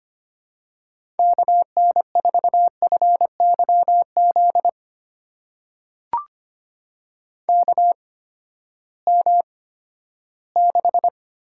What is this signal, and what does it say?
Morse code, heard as a keyed tone, about 25 wpm, spaced as for 14 wpm, 700 Hz. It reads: KN4FYZ KM6
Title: USA Call Signs - 6 Characters - 8x Character Spacing - 25wpm